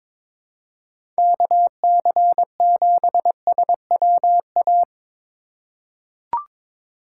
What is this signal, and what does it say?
Morse code, heard as a keyed tone, about 22 wpm, 700 Hz. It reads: KC7SWA
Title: USA Call Signs - 6 Characters - 3x Character Spacing - 22wpm